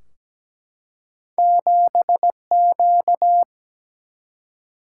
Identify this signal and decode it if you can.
Morse code, heard as a keyed tone, about 17 wpm, 700 Hz. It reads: 7Q
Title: Single-Letter Single-Number 17wpm